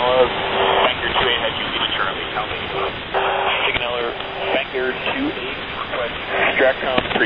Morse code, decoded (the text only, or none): none